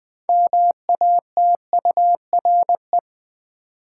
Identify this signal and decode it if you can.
Morse code, heard as a keyed tone, about 20 wpm, 700 Hz. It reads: MATURE
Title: 6 Letter Words - First 3 Letters Given - 20wpm